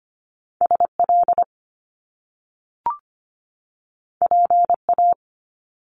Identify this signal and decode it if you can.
Morse code, heard as a keyed tone, about 25 wpm, 700 Hz. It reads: SL PA
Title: Top Two Letter Combinations in Words 25wpm